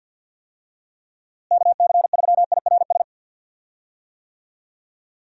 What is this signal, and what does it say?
Morse code, heard as a keyed tone, about 51 wpm, 700 Hz. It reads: KC3IRS
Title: USA Call Signs - 6 Characters - Repeated 3x - 50wpm